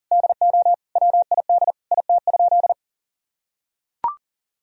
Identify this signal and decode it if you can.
Morse code, heard as a keyed tone, about 37 wpm, 700 Hz. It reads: DO WID IT?
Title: Sentences from Top 100 Words and Abbreviations 40wpm